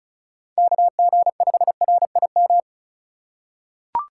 Morse code, heard as a keyed tone, about 36 wpm, 700 Hz. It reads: KG5RIM
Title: USA Call Signs - 6 Characters - 4x Character Spacing - 35wpm